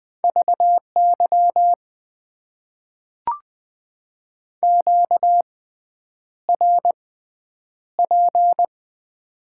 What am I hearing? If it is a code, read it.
Morse code, heard as a keyed tone, about 20 wpm, spaced as for 9 wpm, 700 Hz. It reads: VY QRP
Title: Top 90 QSO Elements - 6x Character Spacing - 20wpm